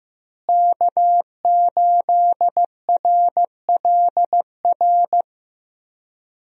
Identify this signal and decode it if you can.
Morse code, heard as a keyed tone, about 15 wpm, 700 Hz. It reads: K8RLR
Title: USA Call Signs - 5 Characters - Repeated 3x - 15wpm